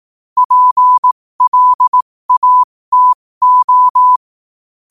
Morse code, heard as a keyed tone, about 18 wpm, spaced as for 15 wpm, 1000 Hz. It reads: PLATO
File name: qotdfiles / 15wpm